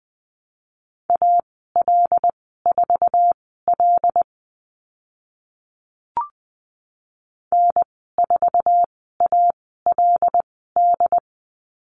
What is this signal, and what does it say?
Morse code, heard as a keyed tone, about 20 wpm, spaced as for 14 wpm, 700 Hz. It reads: AL4L N4ALD
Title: 082 Lesson 14g - Number 2 - Call Signs 20wpm